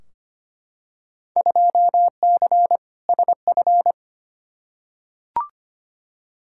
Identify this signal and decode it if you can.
Morse code, heard as a keyed tone, about 25 wpm, 700 Hz. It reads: 2C SF
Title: Field Day Exchanges Encore - 25wpm